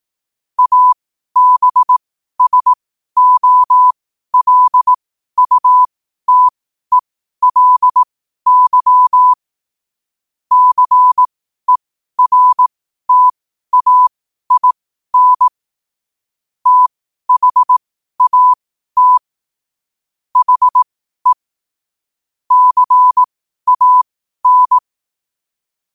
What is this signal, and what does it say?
Morse code, heard as a keyed tone, about 18 wpm, spaced as for 12 wpm, 1000 Hz. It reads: ABSOLUTELY CERTAIN THAT HE CAN
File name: qotdfiles / 12wpm